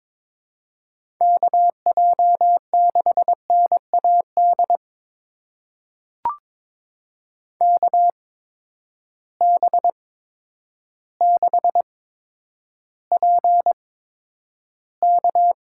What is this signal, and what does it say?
Morse code, heard as a keyed tone, about 22 wpm, spaced as for 10 wpm, 700 Hz. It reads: KJ6NAD KB6PK
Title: USA Call Signs - 6 Characters - 8x Character Spacing - 22wpm